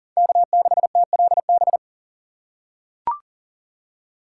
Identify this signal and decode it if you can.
Morse code, heard as a keyed tone, about 40 wpm, 700 Hz. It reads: K6TLB
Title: USA Call Signs - 5 Characters - 3x Character Spacing - 40wpm